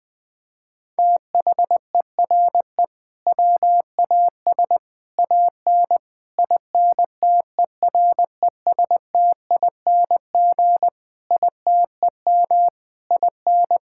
Morse code, heard as a keyed tone, about 20 wpm, 700 Hz. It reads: THERE WAS AN INTERESTING ITEM IN
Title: Sentences from Top 1000 Words - Review Part 6 - 20wpm